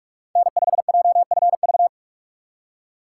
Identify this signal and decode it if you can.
Morse code, heard as a keyed tone, about 47 wpm, 700 Hz. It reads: N5JFV
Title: USA Call Signs - 4x Character Spacing - 45wpm